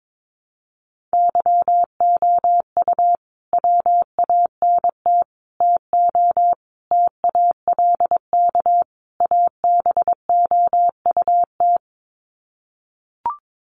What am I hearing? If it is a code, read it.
Morse code, heard as a keyed tone, about 22 wpm, 700 Hz. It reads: YOU WANT TO TALK ABOUT
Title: Sets of 5 Words Encore 22wpm